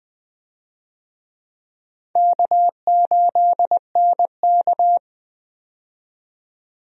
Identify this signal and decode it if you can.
Morse code, heard as a keyed tone, about 20 wpm, 700 Hz. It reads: K8NK
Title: USA Call Signs - 4 Characters - 20wpm